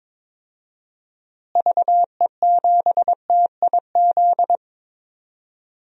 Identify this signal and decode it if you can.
Morse code, heard as a keyed tone, about 22 wpm, 700 Hz. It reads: VE7TIZ
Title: DXCC-100 Call Signs - 6 Characters - Repeated 2x - 22wpm